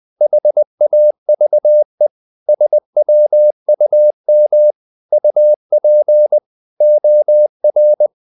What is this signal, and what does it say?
Morse code, heard as a keyed tone, about 20 wpm, 600 Hz. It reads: HAVE SWUM UP OR
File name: the old man and the sea / the old man and the sea part 1 20wpm